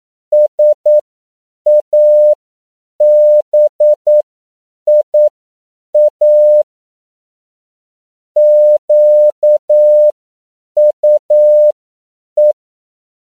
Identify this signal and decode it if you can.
Morse code, heard as a keyed tone, about 9 wpm, spaced as for 7 wpm, 600 Hz. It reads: SABIA QUE
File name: CW / 10PPM